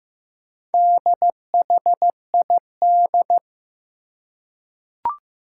Morse code, heard as a keyed tone, about 15 wpm, 700 Hz. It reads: DHID
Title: Four Challenging Letter-Number Combinations 15wpm